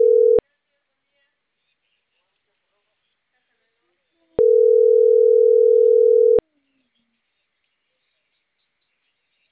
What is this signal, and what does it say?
Ringback tone (North America), 440 + 480 Hz.